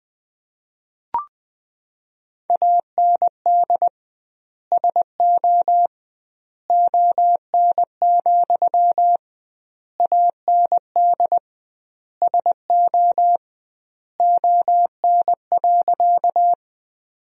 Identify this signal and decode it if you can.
Morse code, heard as a keyed tone, about 20 wpm, 700 Hz. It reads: AND SO ON, AND SO ON.
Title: 2x Word-Spacing - Sentences from Top 100 Words - 20wpm